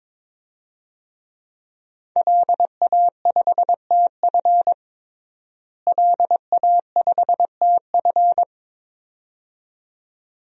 Morse code, heard as a keyed tone, about 22 wpm, 700 Hz. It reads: LA5TF LA5TF
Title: DXCC-100 Call Signs - 5 Characters - Repeated 2x - 22wpm